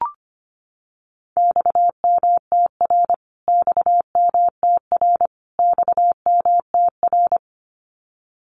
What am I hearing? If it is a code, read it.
Morse code, heard as a keyed tone, about 25 wpm, 700 Hz. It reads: XMTR XMTR XMTR
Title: Abbreviations Repeated 3x - 25wpm